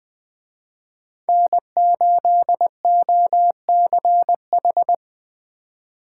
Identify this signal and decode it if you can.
Morse code, heard as a keyed tone, about 20 wpm, 700 Hz. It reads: N8OCH